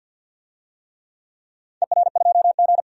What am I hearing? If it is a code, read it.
Morse code, heard as a keyed tone, about 53 wpm, 700 Hz. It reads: ER2G
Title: DXCC-100 Call Signs - 4 Characters - 50wpm